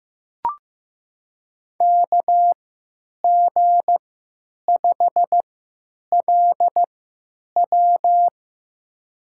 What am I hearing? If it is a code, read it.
Morse code, heard as a keyed tone, about 15 wpm, spaced as for 9 wpm, 700 Hz. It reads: KG5LW
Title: USA Call Signs - 5 Characters - 3x Character Spacing - 15wpm